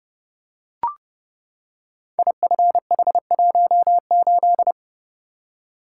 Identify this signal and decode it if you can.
Morse code, heard as a keyed tone, about 30 wpm, 700 Hz. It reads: IFH18
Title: Code Groups - 5 Letter:Numbers - 30wpm